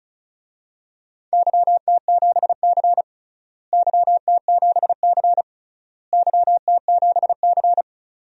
Morse code, heard as a keyed tone, about 36 wpm, 700 Hz. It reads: YT7C YT7C YT7C
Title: DXCC-100 Call Signs - 4 Characters - Repeated 3x - 35wpm